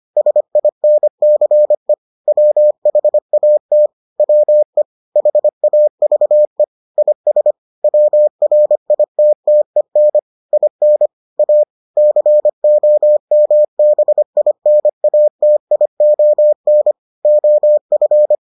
Morse code, heard as a keyed tone, about 25 wpm, 600 Hz. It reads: SINCE WHAT WE HAVE IS WRITTEN IN A COMBINATION OF